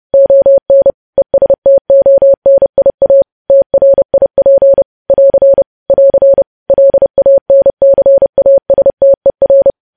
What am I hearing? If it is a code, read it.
Morse code, heard as a keyed tone, about 30 wpm, 550 Hz. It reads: ON ESTONIA TRIP <AR> <AR> LANCASTER
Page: news headlines in morse code at 30 wpm